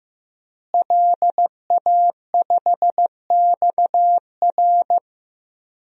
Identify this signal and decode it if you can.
Morse code, heard as a keyed tone, about 15 wpm, 700 Hz. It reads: LA5XR
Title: DXCC-100 Call Signs - 5 Characters - Repeated 3x - 15wpm